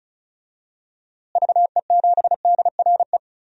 Morse code, heard as a keyed tone, about 34 wpm, 700 Hz. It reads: VE7DRE
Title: DXCC-100 Call Signs - 6 Characters - Repeated 2x - 35wpm